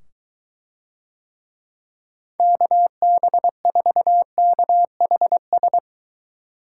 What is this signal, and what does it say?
Morse code, heard as a keyed tone, about 23 wpm, 700 Hz. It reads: KB4KHS